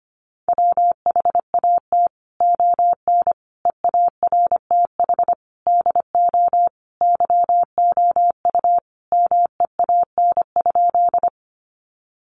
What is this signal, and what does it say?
Morse code, heard as a keyed tone, about 25 wpm, 700 Hz. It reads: WHAT ON EARTH DO YOU MEAN?
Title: Sentences from Top 200 Words 25wpm